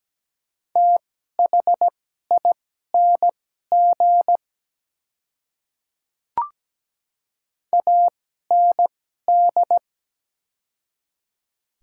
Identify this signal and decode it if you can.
Morse code, heard as a keyed tone, about 17 wpm, spaced as for 12 wpm, 690 Hz. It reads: THING AND
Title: 2x Character Spacing - Top 100 Words - 17wpm